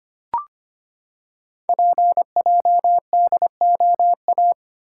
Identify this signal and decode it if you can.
Morse code, heard as a keyed tone, about 25 wpm, 700 Hz. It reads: PJDOA
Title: Code Groups - 5 Letters - 25wpm